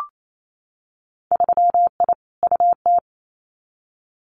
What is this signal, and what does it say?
Morse code, heard as a keyed tone, about 28 wpm, 700 Hz. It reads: 3I UT
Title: Winter Field Day Exchanges Encore 28wpm